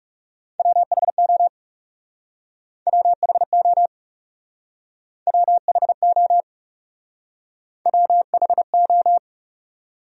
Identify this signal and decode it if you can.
Morse code, heard as a keyed tone, about 40 wpm, 700 Hz. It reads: WHO WHO WHO WHO